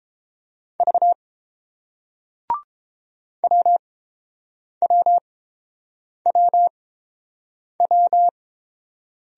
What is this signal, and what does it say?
Morse code, heard as a keyed tone, about 30 wpm, spaced as for 7 wpm, 700 Hz. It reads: V WWWW